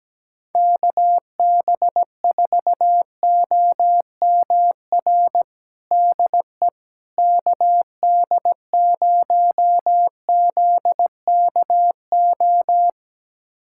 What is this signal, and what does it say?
Morse code, heard as a keyed tone, about 17 wpm, 700 Hz. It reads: KB4OMR DE KD0ZKO
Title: USA Callsign DE Callsign - 6 Characters - 17wpm